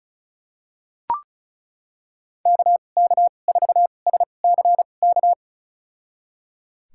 Morse code, heard as a keyed tone, about 34 wpm, spaced as for 25 wpm, 700 Hz. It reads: KK4SCK
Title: USA Call Signs - 6 Characters - 2x Character Spacing - 35wpm